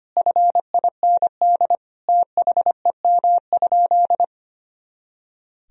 Morse code, heard as a keyed tone, about 25 wpm, 700 Hz. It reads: FIND THEM?
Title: Sentences from Top 200 Words and Abbreviations 25wpm